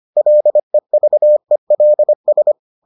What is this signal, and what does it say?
Morse code, heard as a keyed tone, about 25 wpm, 600 Hz. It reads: LEVELS